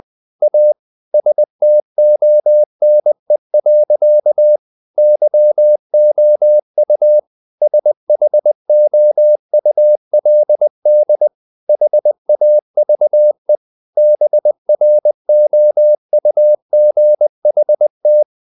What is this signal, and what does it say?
Morse code, heard as a keyed tone, about 20 wpm, 600 Hz. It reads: A STONE. YOU SHOULD HAVE BROUGHT
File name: the old man and the sea part 2 20wpm